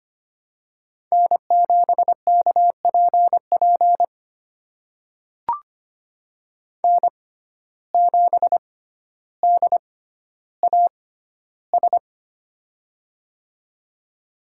Morse code, heard as a keyed tone, about 25 wpm, spaced as for 13 wpm, 700 Hz. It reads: N7KPP N7DAS
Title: USA Call Signs - 5 Characters - 6x Character Spacing - 25wpm